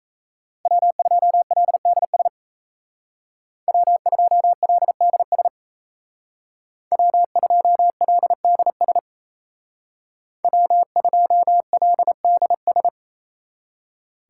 Morse code, heard as a keyed tone, about 36 wpm, 700 Hz. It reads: W2LDS W2LDS W2LDS W2LDS